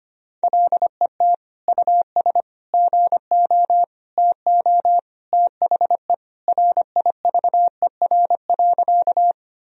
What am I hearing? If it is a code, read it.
Morse code, heard as a keyed tone, about 25 wpm, 700 Hz. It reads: LET US GO TO THE RIVER.
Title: Sentences from Top 1000 Words - Review Part 1 - 25wpm